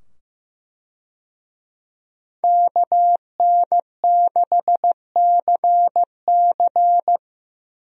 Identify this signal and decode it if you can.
Morse code, heard as a keyed tone, about 15 wpm, 700 Hz. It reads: KN6CC
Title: USA Call Signs - 5 Characters - 15wpm